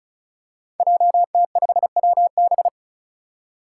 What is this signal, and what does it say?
Morse code, heard as a keyed tone, about 34 wpm, 700 Hz. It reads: JT5WB
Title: Five Challenging Letter-Number Combinations 35wpm